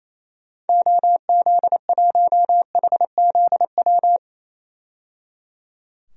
Morse code, heard as a keyed tone, about 28 wpm, 700 Hz. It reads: OZ1HZW
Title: DXCC-100 Call Signs - 6 Characters - 28wpm